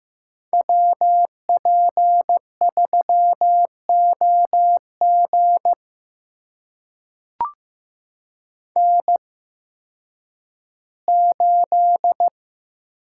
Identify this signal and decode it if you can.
Morse code, heard as a keyed tone, about 15 wpm, spaced as for 10 wpm, 700 Hz. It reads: WP3OG N8
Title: USA Call Signs - 5 Characters - 8x Character Spacing - 15wpm